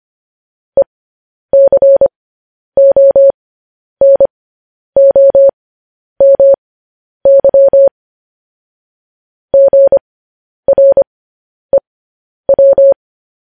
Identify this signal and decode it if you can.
Morse code, heard as a keyed tone, about 25 wpm, spaced as for 10 wpm, 550 Hz. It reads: ECONOMY GREW
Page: news headlines in morse code at 10 wpm